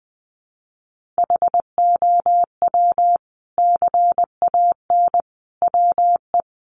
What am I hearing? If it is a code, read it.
Morse code, heard as a keyed tone, about 20 wpm, 700 Hz. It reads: HOW CAN WE
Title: Sentences from Top 1400 Words 20wpm